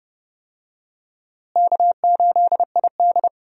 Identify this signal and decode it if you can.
Morse code, heard as a keyed tone, about 30 wpm, 700 Hz. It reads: K8ID
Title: USA Call Signs - 4 Characters - Repeated 2x - 30wpm